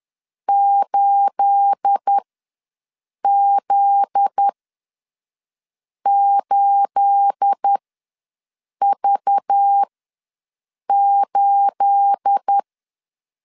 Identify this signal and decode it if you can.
Morse code, heard as a keyed tone, about 10 wpm, spaced as for 5 wpm, 800 Hz. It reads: 8Z8V8
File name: LEZIONI / 9 28 ws 10wpm 4124